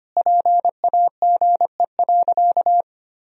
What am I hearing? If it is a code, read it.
Morse code, heard as a keyed tone, about 25 wpm, 700 Hz. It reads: PAGE.
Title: Sentences from Top 1000 Words - Review Part 1 - 25wpm